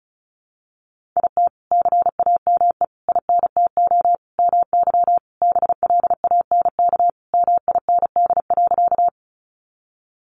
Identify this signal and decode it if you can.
Morse code, heard as a keyed tone, about 34 wpm, 700 Hz. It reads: IT CAME INTO MY BLANK MIND.
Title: Sentences from Top 1500 Words 35wpm